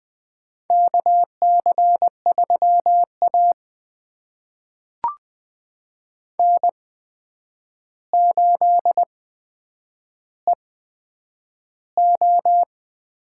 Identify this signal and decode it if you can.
Morse code, heard as a keyed tone, about 20 wpm, spaced as for 9 wpm, 700 Hz. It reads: KC3A N8EO
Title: USA Call Signs - 8x Character Spacing - 20wpm